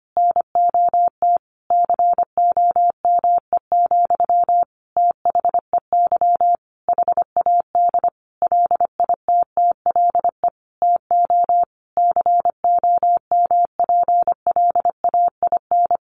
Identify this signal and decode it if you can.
Morse code, heard as a keyed tone, about 25 wpm, 700 Hz. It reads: NOT COME, THEY HAD LITTLE TO COMPLAIN
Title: Sentences from Top 1300 Words 25wpm